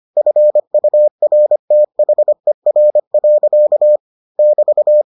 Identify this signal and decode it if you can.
Morse code, heard as a keyed tone, about 25 wpm, 600 Hz. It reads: FURTHER. <BT>